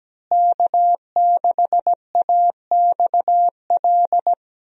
Morse code, heard as a keyed tone, about 17 wpm, 700 Hz. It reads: K6AXL
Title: USA Call Signs - 5 Characters - Repeated 3x - 17wpm